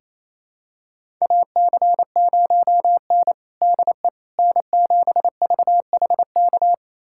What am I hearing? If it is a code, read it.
Morse code, heard as a keyed tone, about 28 wpm, 700 Hz. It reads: AC0N DE N7VHK
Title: USA Callsign DE Callsign 28wpm